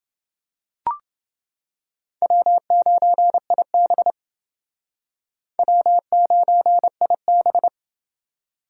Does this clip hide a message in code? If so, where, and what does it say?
Yes: Morse code from 2.2 to 7.7 s, heard as a keyed tone, about 29 wpm, 700 Hz. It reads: W9IB W9IB